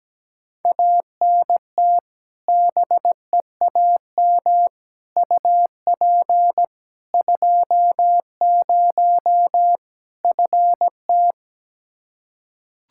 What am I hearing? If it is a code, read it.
Morse code, heard as a keyed tone, about 17 wpm, 700 Hz. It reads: ANT BEAM UP 20 FT
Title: K1USN SST QSO Practice 17wpm